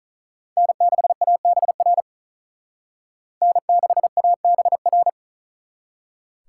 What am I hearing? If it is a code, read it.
Morse code, heard as a keyed tone, about 45 wpm, 700 Hz. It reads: N6ABR N6ABR